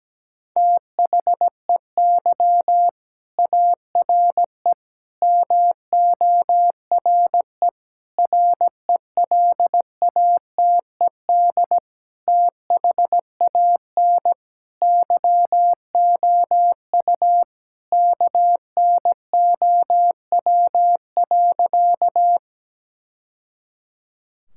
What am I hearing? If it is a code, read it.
Morse code, heard as a keyed tone, about 17 wpm, 700 Hz. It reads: THEY ARE MORE RELATED THAN YOU KNOW.
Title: Sentences from Top 1000 Words - Review Part 5 - 17wpm